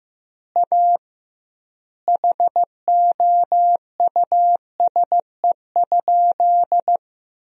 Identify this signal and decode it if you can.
Morse code, heard as a keyed tone, about 15 wpm, 700 Hz. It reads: A HOUSE?
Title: 2x Word-Spacing - Sentences from Top 200 Words - 15wpm